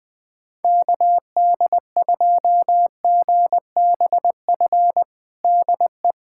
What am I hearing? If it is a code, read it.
Morse code, heard as a keyed tone, about 21 wpm, 700 Hz. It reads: KD2GBF DE
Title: USA Callsign DE Callsign 20wpm